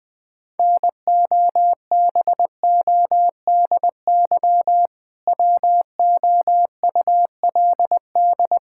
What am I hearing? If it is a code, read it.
Morse code, heard as a keyed tone, about 21 wpm, 700 Hz. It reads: NOBODY WOULD
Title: Sentences from Top 1600 Words 20wpm